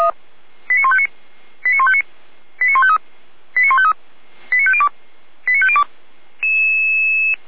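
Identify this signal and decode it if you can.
Five-tone selective call (ZVEI), heard as quick runs of five tones, tones of about 70 ms. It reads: 89159 89159 89153 89153 89502 89502
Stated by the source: ZVEI five-tone selective call received by SDR, fire brigade (Feuerwehr) Orlamünde.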